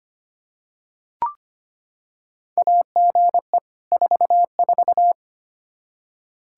Morse code, heard as a keyed tone, about 25 wpm, 700 Hz. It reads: AGE 44
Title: Practice QSO - Exchange 3 - More Info - 25wpm